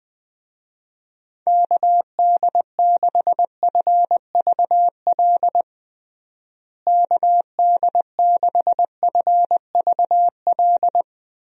Morse code, heard as a keyed tone, about 20 wpm, 700 Hz. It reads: KD6FVL KD6FVL
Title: USA Call Signs - 6 Characters - Repeated 2x - 20wpm